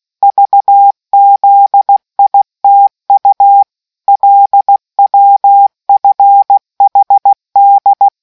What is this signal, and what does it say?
Morse code, heard as a keyed tone, about 16 wpm, 800 Hz. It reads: VZITU LWFHD